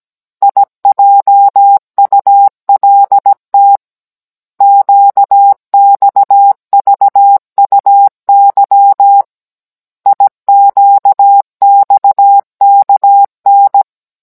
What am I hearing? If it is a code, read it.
Morse code, heard as a keyed tone, about 17 wpm, 800 Hz. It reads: IJULT QXVUY IQXKN
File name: Cours-NR19-16mm